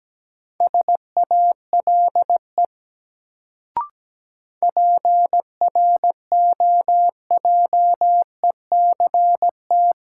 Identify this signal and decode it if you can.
Morse code, heard as a keyed tone, about 17 wpm, 700 Hz. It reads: SALE PROJECT
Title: Sentences from Top 1400 Words - New Word Drill - 17wpm